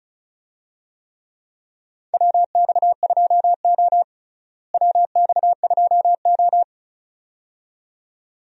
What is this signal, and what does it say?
Morse code, heard as a keyed tone, about 34 wpm, 700 Hz. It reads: WX2O WX2O
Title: USA Call Signs - 4 Characters - Repeated 2x - 35wpm